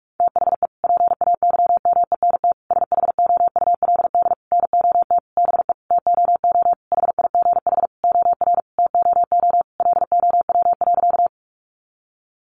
Morse code, heard as a keyed tone, about 44 wpm, 700 Hz. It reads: THE PAYMENT SHOULD NOT BE TOO HIGH OR TOO LOW.
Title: Sentences from Top 1200 Words 45wpm